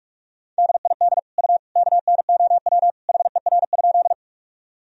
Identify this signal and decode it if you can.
Morse code, heard as a keyed tone, about 44 wpm, 700 Hz. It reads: DID U KNOW HER?
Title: Sentences from Top 100 Words and Abbreviations 45wpm